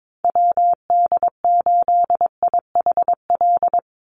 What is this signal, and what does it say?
Morse code, heard as a keyed tone, about 22 wpm, 700 Hz. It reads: WD8IHL
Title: USA Call Signs 22wpm